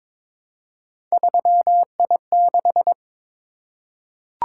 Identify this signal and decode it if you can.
Morse code, heard as a keyed tone, about 22 wpm, 700 Hz. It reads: 3I6
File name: practice / 22wpm / Three Letter-Number Combinations - 22wpm